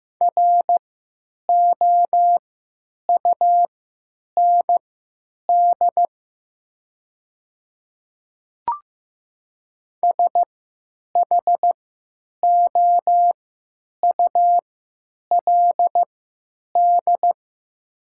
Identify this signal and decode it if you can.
Morse code, heard as a keyed tone, about 15 wpm, spaced as for 9 wpm, 700 Hz. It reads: ROUND SHOULD
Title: 3x Character Spacing - Top 300 Words - 15wpm